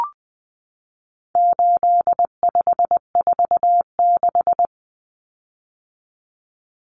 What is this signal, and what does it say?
Morse code, heard as a keyed tone, about 20 wpm, 700 Hz. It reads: 8546
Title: Common Cut Numbers - 4-Digit 20wpm